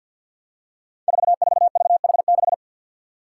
Morse code, heard as a keyed tone, about 53 wpm, 700 Hz. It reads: 44VH6
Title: Five Challenging Letter-Number Combinations 50wpm